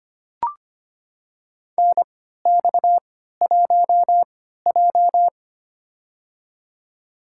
Morse code, heard as a keyed tone, about 25 wpm, spaced as for 14 wpm, 700 Hz. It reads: NX1J
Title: USA Call Signs - 3x Character Spacing - 25wpm